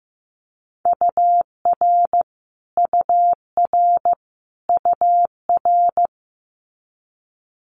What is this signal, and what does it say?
Morse code, heard as a keyed tone, about 15 wpm, 700 Hz. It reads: UR UR UR
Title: Multi-letter Phonemes Repeated 3x 15wpm